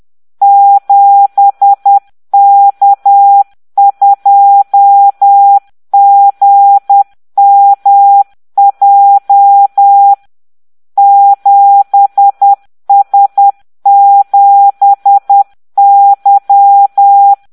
Morse code, heard as a keyed tone, about 10 wpm, 800 Hz. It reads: 7K2GMJ 7S7Y